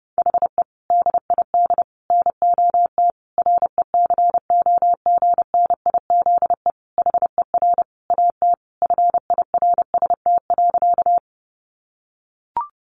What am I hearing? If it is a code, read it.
Morse code, heard as a keyed tone, about 30 wpm, 700 Hz. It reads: HE DID NOT RECOGNIZE HER AT FIRST.
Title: Sentences from Top 1400 Words 30wpm